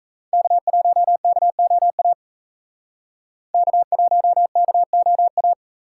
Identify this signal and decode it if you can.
Morse code, heard as a keyed tone, about 34 wpm, 700 Hz. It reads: K1KOA K1KOA
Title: Speed Racing - USA 1x3 Call Signs 28wpm